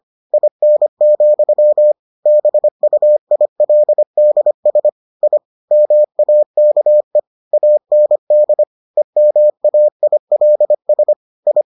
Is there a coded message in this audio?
Yes: Morse code from 0.3 to 11.6 s, heard as a keyed tone, about 25 wpm, 600 Hz. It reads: IN, BUILDS I MAKE AND EMAILS I